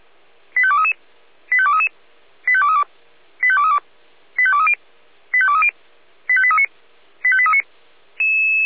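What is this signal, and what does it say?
Five-tone selective call (ZVEI), heard as quick runs of five tones, tones of about 70 ms. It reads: 85233 85233 85232 85232 85230 85230 85839 85839
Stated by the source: ZVEI five-tone call 85830 received on an SDR.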